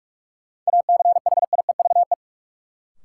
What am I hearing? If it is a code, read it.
Morse code, heard as a keyed tone, about 46 wpm, 700 Hz. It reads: ACHIEVE
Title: Sentences from Top 1000 Words - Review Part 5 - New Word Drill - 45wpm